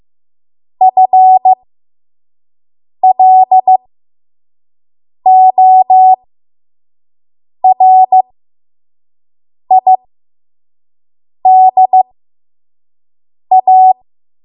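Morse code, heard as a keyed tone, about 15 wpm, spaced as for 5 wpm, 750 Hz. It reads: FLORIDA
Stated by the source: 5 WPM Morse code practice.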